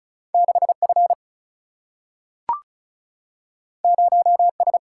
Morse code, heard as a keyed tone, about 36 wpm, 700 Hz. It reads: 6 F 0 S
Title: Single-Letter Single-Number 35wpm